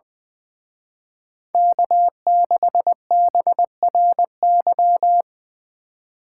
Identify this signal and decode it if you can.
Morse code, heard as a keyed tone, about 24 wpm, 700 Hz. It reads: K6BRY
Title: Speed Racing - USA 1x3 Call Signs 20wpm